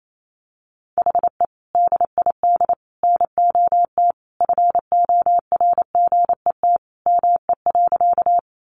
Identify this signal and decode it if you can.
Morse code, heard as a keyed tone, about 28 wpm, 700 Hz. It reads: HE DID NOT FORGET ME.
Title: Sentences from Top 1200 Words 28wpm